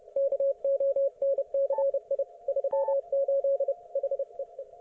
Morse code, heard as a keyed tone, about 30 wpm, 550 Hz. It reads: KONCI 38 HEE
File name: Archiv / CUC181 / ok1fhi cul181